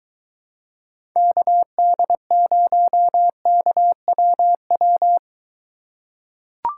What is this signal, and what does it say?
Morse code, heard as a keyed tone, about 21 wpm, 700 Hz. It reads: KD0KWW